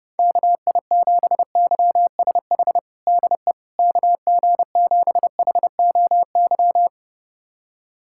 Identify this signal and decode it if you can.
Morse code, heard as a keyed tone, about 32 wpm, 700 Hz. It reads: KI7YSH DE KG7HOY
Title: USA Callsign DE Callsign 30wpm